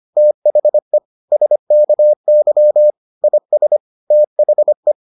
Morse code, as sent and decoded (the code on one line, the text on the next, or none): - .... / ... -.- -.-- / .. ... / - .... .
THE SKY IS THE